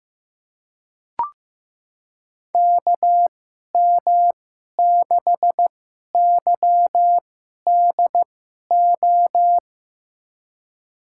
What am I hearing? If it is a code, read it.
Morse code, heard as a keyed tone, about 15 wpm, spaced as for 11 wpm, 700 Hz. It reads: KM6YDO
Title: USA Call Signs - 6 Characters - 2x Character Spacing - 15wpm